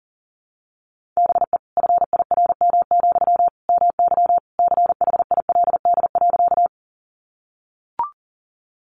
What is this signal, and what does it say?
Morse code, heard as a keyed tone, about 40 wpm, 700 Hz. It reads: BE FIRM, MY CHILD.
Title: Sentences from Top 1400 Words 40wpm